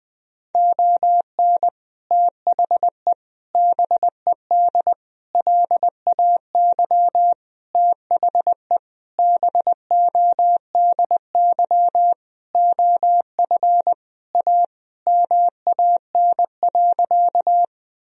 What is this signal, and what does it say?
Morse code, heard as a keyed tone, about 20 wpm, 700 Hz. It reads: ON THE BED LAY THE BODY OF A MAN.
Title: Sentences from Top 1000 Words - Review Part 2 - 20wpm